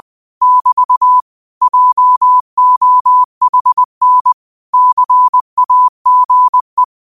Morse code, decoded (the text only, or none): <BT> JOHN CAGE